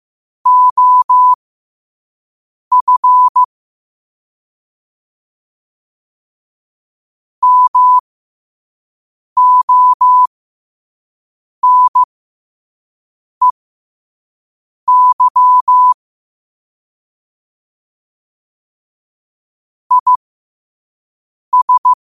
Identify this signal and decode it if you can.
Morse code, heard as a keyed tone, about 15 wpm, spaced as for 5 wpm, 1000 Hz. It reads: OF MONEY IS
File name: qotdfiles / 5wpm